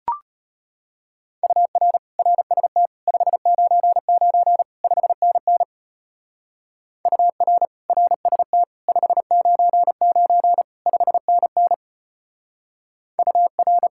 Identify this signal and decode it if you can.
Morse code, heard as a keyed tone, about 31 wpm, 700 Hz. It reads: UR RST 599 5NN UR RST 599 5NN UR